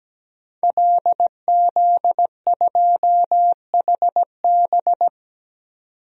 Morse code, heard as a keyed tone, about 17 wpm, 700 Hz. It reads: LZ2HB
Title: DXCC-100 Call Signs - 5 Characters - 17wpm